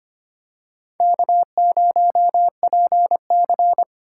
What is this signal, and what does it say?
Morse code, heard as a keyed tone, about 25 wpm, 700 Hz. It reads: K0PC
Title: K1USN SST QSO Practice - 25wpm